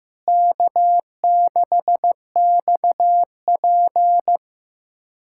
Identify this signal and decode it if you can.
Morse code, heard as a keyed tone, about 15 wpm, 700 Hz. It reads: K6XP
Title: USA Callsign DE Callsign - 4 Characters - 15wpm